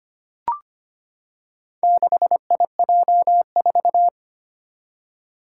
Morse code, heard as a keyed tone, about 25 wpm, 700 Hz. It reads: 6IJ4
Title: Four Challenging Letter-Number Combinations 25wpm